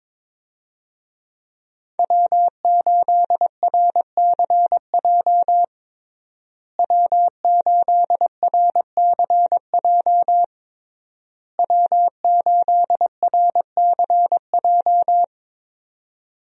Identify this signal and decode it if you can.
Morse code, heard as a keyed tone, about 22 wpm, 700 Hz. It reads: W8RCJ W8RCJ W8RCJ
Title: USA Call Signs - 5 Characters - Repeated 3x - 22wpm